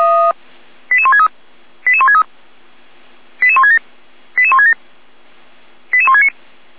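Five-tone selective call (ZVEI), heard as quick runs of five tones, tones of about 70 ms. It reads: 88163 88163 88167 88167 88169
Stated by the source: ZVEI five-tone call received by SDR.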